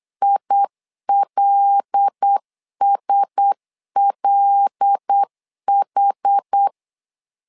Morse code, heard as a keyed tone, about 8 wpm, 800 Hz. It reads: ILSLH
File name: LEZIONI / 5 LBVC / 5 L 8wpm 2831